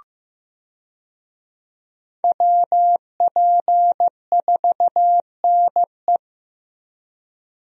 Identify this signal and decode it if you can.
Morse code, heard as a keyed tone, about 16 wpm, 700 Hz. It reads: WP4NE
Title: USA Call Signs - 5 Characters - Repeated 2x - 15wpm